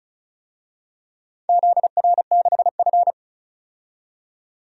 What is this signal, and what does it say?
Morse code, heard as a keyed tone, about 36 wpm, 700 Hz. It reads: ZR6F
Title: DXCC-100 Call Signs - 4 Characters - 35wpm